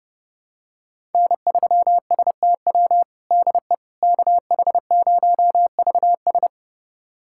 Morse code, heard as a keyed tone, about 30 wpm, 700 Hz. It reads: N3STW DE KH0VS
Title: USA Callsign DE Callsign - 5 Characters - 30wpm